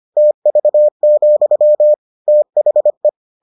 Morse code, heard as a keyed tone, about 25 wpm, 600 Hz. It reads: TV, THE